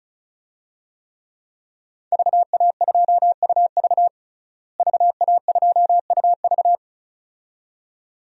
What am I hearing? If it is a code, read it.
Morse code, heard as a keyed tone, about 36 wpm, 700 Hz. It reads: VA2UV VA2UV
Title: DXCC-100 Call Signs - 5 Characters - Repeated 2x - 35wpm